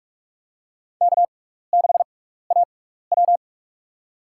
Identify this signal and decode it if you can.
Morse code, heard as a keyed tone, about 47 wpm, spaced as for 16 wpm, 700 Hz. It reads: K6AW
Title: USA Call Signs - 4 Characters - 6x Character Spacing - 45wpm